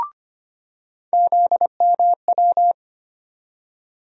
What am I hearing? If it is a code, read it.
Morse code, heard as a keyed tone, about 25 wpm, 700 Hz. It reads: ZMW
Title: USA Call Sign Postfixes - 3 Letters 25wpm